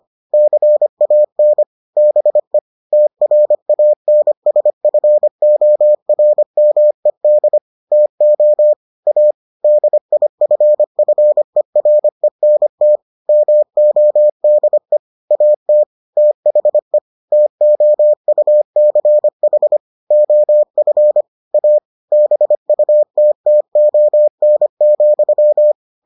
Morse code, heard as a keyed tone, about 25 wpm, 600 Hz. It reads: CAN BE TRANSFORMED TO A DIFFERENT MODE AT THE TOUCH OF A BUTTON,